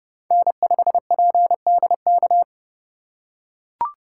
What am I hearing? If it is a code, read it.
Morse code, heard as a keyed tone, about 31 wpm, 700 Hz. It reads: N5PDK
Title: USA Call Signs - 5 Characters - 3x Character Spacing - 30wpm